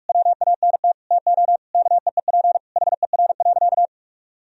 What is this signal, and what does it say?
Morse code, heard as a keyed tone, about 46 wpm, 700 Hz. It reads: WANT TO KEEP HER.
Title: Sentences from Top 1000 Words - Review Part 2 - 45wpm